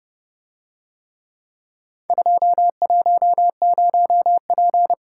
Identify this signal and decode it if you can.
Morse code, heard as a keyed tone, about 28 wpm, 700 Hz. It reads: 210P